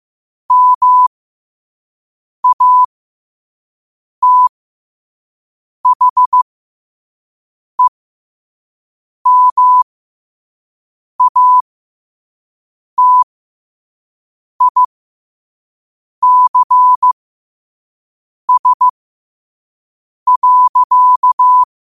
Morse code, heard as a keyed tone, about 15 wpm, spaced as for 5 wpm, 1000 Hz. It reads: MATHEMATICS.
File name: qotdfiles / 5wpm